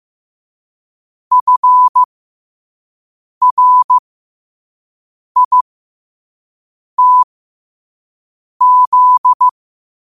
Morse code, heard as a keyed tone, about 15 wpm, spaced as for 5 wpm, 1000 Hz. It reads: FRITZ
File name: qotdfiles / 5wpm